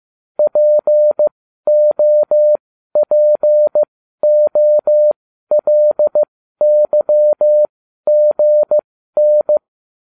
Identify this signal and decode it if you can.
Morse code, heard as a keyed tone, about 15 wpm, spaced as for 12 wpm, 600 Hz. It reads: POPOLYGN